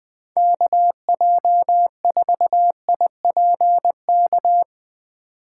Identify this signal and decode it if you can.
Morse code, heard as a keyed tone, about 20 wpm, 700 Hz. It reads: KJ4IPK
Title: USA Call Signs - 6 Characters - 6x Character Spacing - 20wpm